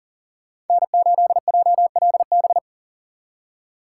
Morse code, heard as a keyed tone, about 49 wpm, 700 Hz. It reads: N8JLB
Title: Speed Racing - USA 1x3 Call Signs 40wpm